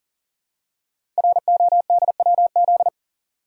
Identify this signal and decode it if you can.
Morse code, heard as a keyed tone, about 41 wpm, 700 Hz. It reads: RODWZ
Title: Code Groups - 5 Letters - 40wpm